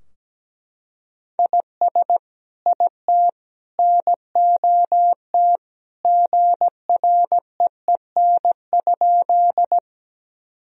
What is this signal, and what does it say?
Morse code, heard as a keyed tone, about 17 wpm, 700 Hz. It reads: IS IT NOT GREEN?
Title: Sentences from Top 1000 Words - Review Part 2 - 17wpm